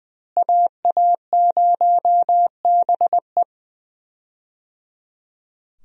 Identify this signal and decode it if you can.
Morse code, heard as a keyed tone, about 20 wpm, 700 Hz. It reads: AA0BE
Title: USA Call Signs - 5 Characters - Repeated 3x - 20wpm